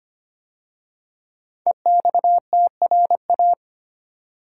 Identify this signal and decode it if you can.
Morse code, heard as a keyed tone, about 25 wpm, 700 Hz. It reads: EXTRA
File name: 5 Letter Words - First 3 Letters Given - 25wpm